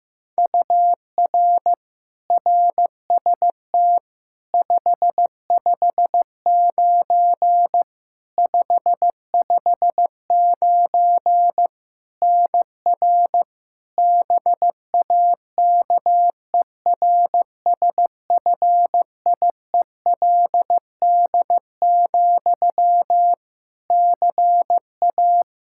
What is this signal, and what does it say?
Morse code, heard as a keyed tone, about 16 wpm, 700 Hz. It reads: UR RST 559 559 NR BAKERSFIELD, CA
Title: Practice QSO - Exchange 1 - Beginning - 15wpm